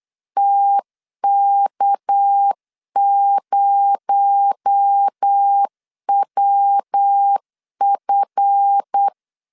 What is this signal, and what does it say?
Morse code, heard as a keyed tone, about 8 wpm, 800 Hz. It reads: TK0WF
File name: LEZIONI / 10 SIGNS / 10D COMA 8wpm 3749